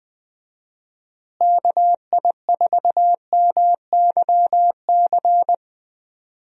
Morse code, heard as a keyed tone, about 20 wpm, 700 Hz. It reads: KI4MYC
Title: USA Call Signs - 6 Characters - Repeated 3x - 20wpm